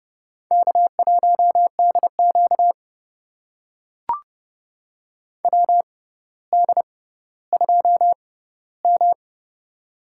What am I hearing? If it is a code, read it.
Morse code, heard as a keyed tone, about 30 wpm, spaced as for 15 wpm, 700 Hz. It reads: K1DQ WD2M
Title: USA Call Signs - 4 Characters - 6x Character Spacing - 30wpm